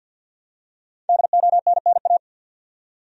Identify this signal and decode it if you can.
Morse code, heard as a keyed tone, about 52 wpm, 700 Hz. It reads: DONNA
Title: USA Common Names - Max 5 Letters - 50wpm